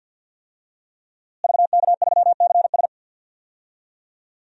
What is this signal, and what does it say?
Morse code, heard as a keyed tone, about 51 wpm, 700 Hz. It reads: VK3CS
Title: DXCC-100 Call Signs - 5 Characters - Repeated 2x - 50wpm